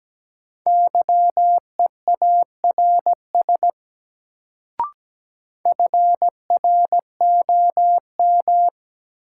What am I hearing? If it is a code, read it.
Morse code, heard as a keyed tone, about 17 wpm, 700 Hz. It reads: YEARS FROM
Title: Sets of 2 Words - New Word Drill - 17wpm